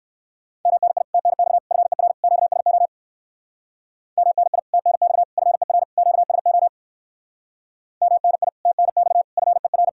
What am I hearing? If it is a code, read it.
Morse code, heard as a keyed tone, about 59 wpm, 700 Hz. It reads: MNI TNX FER QSO MNI TNX FER QSO MNI TNX FER